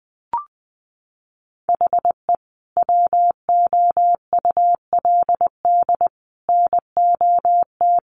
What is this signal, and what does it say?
Morse code, heard as a keyed tone, about 20 wpm, 700 Hz. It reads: HE WOULD NOT
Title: Sentences from Top 1300 Words 20wpm